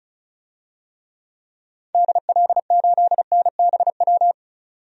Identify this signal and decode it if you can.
Morse code, heard as a keyed tone, about 36 wpm, 700 Hz. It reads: DL8NBW
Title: DXCC-100 Call Signs - 6 Characters - 35wpm